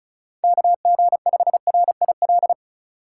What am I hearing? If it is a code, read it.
Morse code, heard as a keyed tone, about 36 wpm, 700 Hz. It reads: KG5RIL
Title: USA Call Signs - 6 Characters - 2x Character Spacing - 35wpm